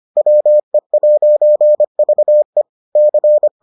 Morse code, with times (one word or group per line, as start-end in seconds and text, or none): 0.2-2.6 WE'VE
3.0-3.5 C